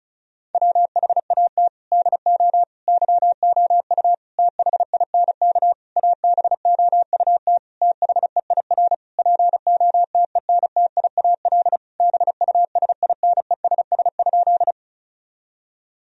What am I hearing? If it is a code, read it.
Morse code, heard as a keyed tone, about 36 wpm, 700 Hz. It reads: WHAT DO YOU THINK ABOUT THEIR POTENTIAL BUSINESS?
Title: Sentences from Top 1000 Words - Review Part 4 - 35wpm